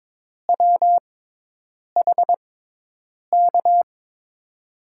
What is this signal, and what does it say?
Morse code, heard as a keyed tone, about 21 wpm, spaced as for 8 wpm, 700 Hz. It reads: WHK
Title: USA Call Signs - 6 Characters - 6x Character Spacing - 22wpm